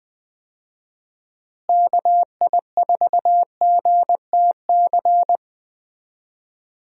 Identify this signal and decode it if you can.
Morse code, heard as a keyed tone, about 21 wpm, 700 Hz. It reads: KI4GTC
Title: USA Call Signs - 6 Characters - Repeated 2x - 20wpm